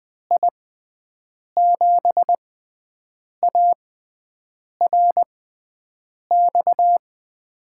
Morse code, heard as a keyed tone, about 20 wpm, spaced as for 7 wpm, 700 Hz. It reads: I7ARX
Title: USA Call Signs - 6 Characters - 6x Character Spacing - 20wpm